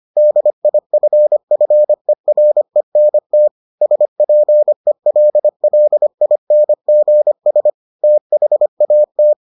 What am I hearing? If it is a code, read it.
Morse code, heard as a keyed tone, about 25 wpm, 600 Hz. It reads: DIFFERENT SPELLINGS THAT